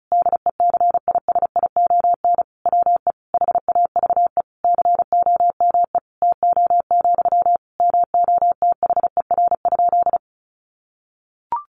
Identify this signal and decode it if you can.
Morse code, heard as a keyed tone, about 34 wpm, 700 Hz. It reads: DECISION WE HAVE COME TO, MOTHER?
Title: Sentences from Top 1000 Words 35wpm